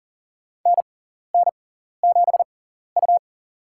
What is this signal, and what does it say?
Morse code, heard as a keyed tone, about 42 wpm, spaced as for 14 wpm, 700 Hz. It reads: NN7U
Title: USA Call Signs - 4 Characters - 6x Character Spacing - 40wpm